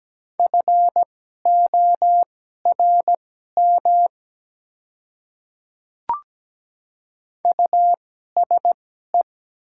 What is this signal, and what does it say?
Morse code, heard as a keyed tone, about 17 wpm, spaced as for 12 wpm, 700 Hz. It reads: FORM USE
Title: 2x Character Spacing - Top 200 Words - 17wpm